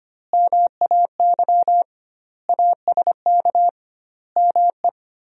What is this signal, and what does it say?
Morse code, heard as a keyed tone, about 25 wpm, 700 Hz. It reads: MAY ASK ME
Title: 2x Word-Spacing - Sentences from Top 200 Words - 25wpm